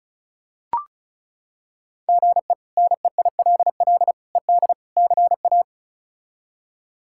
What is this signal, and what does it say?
Morse code, heard as a keyed tone, about 34 wpm, 700 Hz. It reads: GE NEILL ED CA
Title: K1USN SST QSO Practice - 35wpm